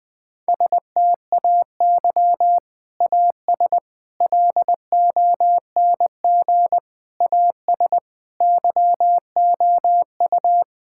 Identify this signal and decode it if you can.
Morse code, heard as a keyed tone, about 20 wpm, 700 Hz. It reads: STAY AS LONG AS YOU
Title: Sentences with 3-Word Phrases - 20wpm